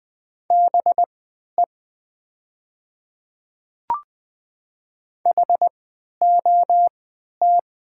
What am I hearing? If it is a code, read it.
Morse code, heard as a keyed tone, about 20 wpm, spaced as for 11 wpm, 700 Hz. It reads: BE HOT
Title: 3x Character Spacing - Top 100 Words - 20wpm